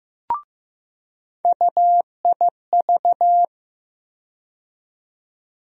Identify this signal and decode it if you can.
Morse code, heard as a keyed tone, about 15 wpm, 700 Hz. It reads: UIV
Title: USA Call Sign Postfixes - 3 Letters 15wpm